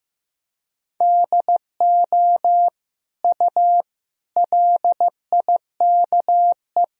Morse code, heard as a keyed tone, about 15 wpm, 700 Hz. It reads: DO U LIKE
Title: Sentences from Top 100 Words and Abbreviations 15wpm